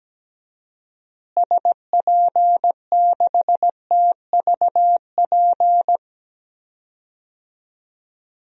Morse code, heard as a keyed tone, about 17 wpm, 700 Hz. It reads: SP6TVP
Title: DXCC-100 Call Signs - 6 Characters - Repeated 2x - 17wpm